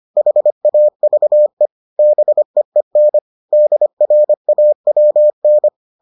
Morse code, heard as a keyed tone, about 25 wpm, 600 Hz. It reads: HAVE BEEN DRAWN